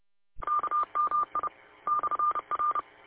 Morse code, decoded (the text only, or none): KMI /R